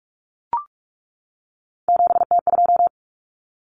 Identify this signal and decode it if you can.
Morse code, heard as a keyed tone, about 44 wpm, 700 Hz. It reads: 7T2
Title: Common Cut Numbers - 3-Digit 45wpm